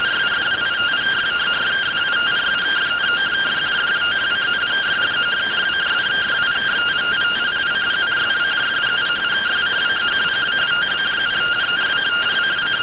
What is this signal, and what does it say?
Two-tone teleprinter signal, about 170 Hz shift.